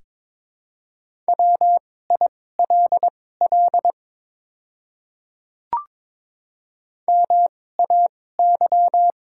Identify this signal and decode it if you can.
Morse code, heard as a keyed tone, about 22 wpm, spaced as for 16 wpm, 690 Hz. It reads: WILL MAY
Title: 2x Character Spacing - Top 200 Words - 22wpm